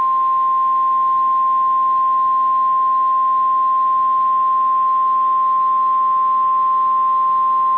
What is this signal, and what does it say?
Emergency Alert System (SAME) tones (the 1050 Hz attention tone). It is heard from 0.6 to 7.1 s.
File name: EAS Logs / KMNR